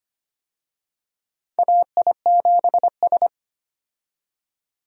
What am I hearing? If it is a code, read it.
Morse code, heard as a keyed tone, about 25 wpm, 700 Hz. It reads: AI7S
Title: USA Call Signs - 4 Characters - 25wpm